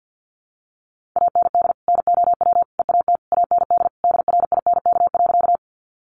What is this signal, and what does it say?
Morse code, heard as a keyed tone, about 50 wpm, 700 Hz. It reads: AND NOW EAT AND DRINK.
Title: Sentences from Top 1500 Words 50wpm